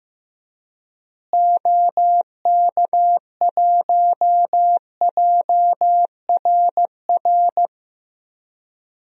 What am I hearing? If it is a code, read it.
Morse code, heard as a keyed tone, about 16 wpm, 700 Hz. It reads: OK1JRR
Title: DXCC-100 Call Signs - 6 Characters - Repeated 3x - 15wpm